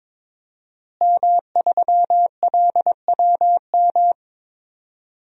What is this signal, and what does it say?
Morse code, heard as a keyed tone, about 22 wpm, 700 Hz. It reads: M3LWM
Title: DXCC-100 Call Signs - 5 Characters - Repeated 2x - 22wpm